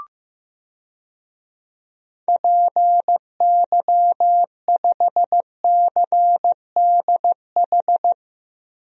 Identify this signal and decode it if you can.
Morse code, heard as a keyed tone, about 15 wpm, 700 Hz. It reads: PY5CDH
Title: DXCC-100 Call Signs - 6 Characters - Repeated 3x - 15wpm